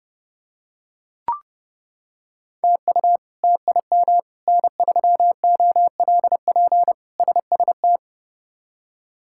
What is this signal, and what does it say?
Morse code, heard as a keyed tone, about 29 wpm, 700 Hz. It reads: TU TIM N3OLP SST
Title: K1USN SST QSO Practice - 30wpm